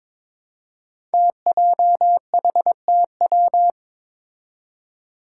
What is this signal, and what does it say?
Morse code, heard as a keyed tone, about 23 wpm, 700 Hz. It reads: TJHTW
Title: Five Challenging Letter-Number Combinations 22wpm